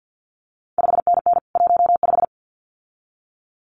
Morse code, heard as a keyed tone, about 50 wpm, 700 Hz. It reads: 5NN 15
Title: CQ WW DX Contest Exchange - 50wpm